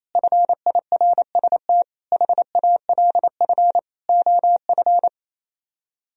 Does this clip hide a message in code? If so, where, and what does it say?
Yes: Morse code from 0.1 to 5.1 s, heard as a keyed tone, about 35 wpm, 700 Hz. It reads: FIRST HALF OF